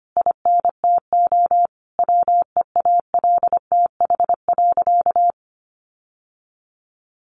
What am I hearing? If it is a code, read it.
Morse code, heard as a keyed tone, about 25 wpm, 700 Hz. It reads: INTO WEALTH.
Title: Sentences from Top 1400 Words 25wpm